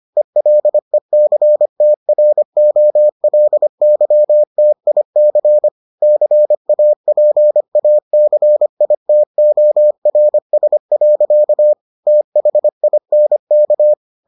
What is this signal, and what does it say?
Morse code, heard as a keyed tone, about 25 wpm, 600 Hz. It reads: ELECTROLYTIC CAPACITORS. THINK